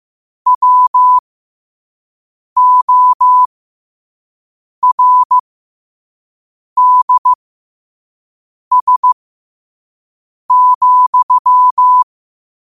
Morse code, heard as a keyed tone, about 15 wpm, spaced as for 5 wpm, 1000 Hz. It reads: WORDS,
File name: qotdfiles / 5wpm